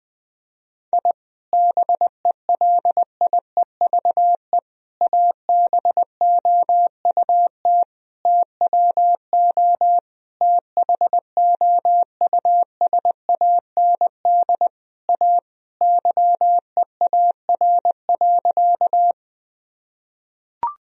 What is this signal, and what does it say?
Morse code, heard as a keyed tone, about 20 wpm, 700 Hz. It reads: I BELIEVE ABOUT TWO THOUSAND A YEAR.
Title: Sentences from Top 1000 Words - Review Part 3 - 20wpm